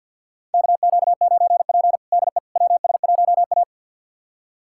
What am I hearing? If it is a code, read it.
Morse code, heard as a keyed tone, about 52 wpm, 700 Hz. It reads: KQ9P DE WS1A
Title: USA Callsign DE Callsign - 4 Characters - 50wpm